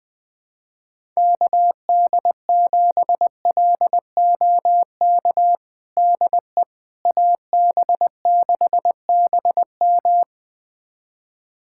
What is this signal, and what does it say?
Morse code, heard as a keyed tone, about 21 wpm, 700 Hz. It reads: KD7LOK DE AB6BM
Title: USA Callsign DE Callsign 20wpm